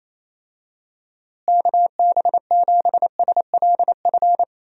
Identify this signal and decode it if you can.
Morse code, heard as a keyed tone, about 27 wpm, 700 Hz. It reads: KB7SLF
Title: USA Call Signs - 6 Characters - Repeated 2x - 28wpm